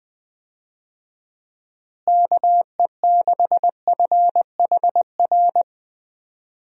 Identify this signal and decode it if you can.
Morse code, heard as a keyed tone, about 21 wpm, 700 Hz. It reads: KE6FHR
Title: USA Call Signs - 6 Characters - Repeated 2x - 20wpm